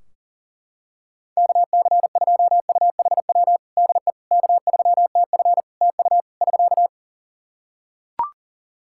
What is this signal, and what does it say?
Morse code, heard as a keyed tone, about 38 wpm, 700 Hz. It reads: KC2UHW DE K3TF TU <SK>
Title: Practice QSO - Exchange 4 - Ending - 40wpm